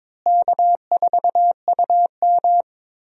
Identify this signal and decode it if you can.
Morse code, heard as a keyed tone, about 22 wpm, 700 Hz. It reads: K4UM
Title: USA Call Signs - 4 Characters - Repeated 3x - 22wpm